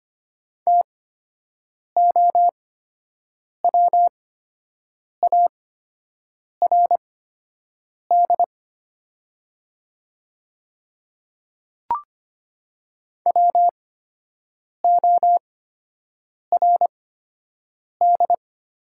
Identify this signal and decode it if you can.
Morse code, heard as a keyed tone, about 25 wpm, spaced as for 7 wpm, 700 Hz. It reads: TOWARD WORD
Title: 8x Character Spacing - Top 500 Words - 25wpm